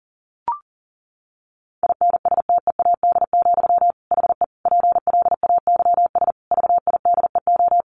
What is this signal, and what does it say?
Morse code, heard as a keyed tone, about 40 wpm, 700 Hz. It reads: INSTEAD, HE PLAYS VIDEO
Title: Additional Easy Phrases 40wpm